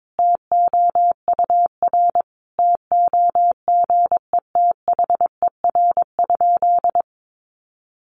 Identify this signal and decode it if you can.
Morse code, heard as a keyed tone, about 22 wpm, 700 Hz. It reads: TOUR TOGETHER?
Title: Sentences from Top 1400 Words 22wpm